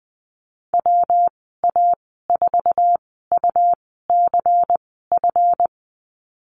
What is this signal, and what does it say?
Morse code, heard as a keyed tone, about 20 wpm, spaced as for 14 wpm, 700 Hz. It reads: WA4UCF
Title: 140 Lesson 23g - Letter f - Call Signs 20wpm